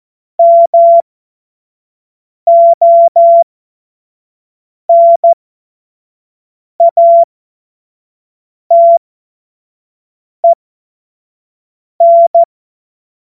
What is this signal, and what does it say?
Morse code, heard as a keyed tone, about 14 wpm, spaced as for 5 wpm, 680 Hz. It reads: MONATEN
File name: Morsekurs / Witze / Witz 31. August 2021, 25 F70